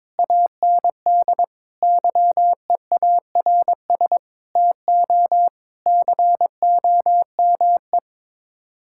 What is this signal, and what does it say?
Morse code, heard as a keyed tone, about 22 wpm, 700 Hz. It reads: AND YEARS TO COME
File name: practice / 22wpm / Sets of 7 Words - 22wpm